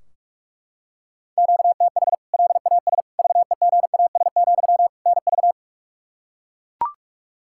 Morse code, heard as a keyed tone, about 42 wpm, 700 Hz. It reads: QTH LAS VEGAS, NV
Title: K1USN SST QSO Practice 45wpm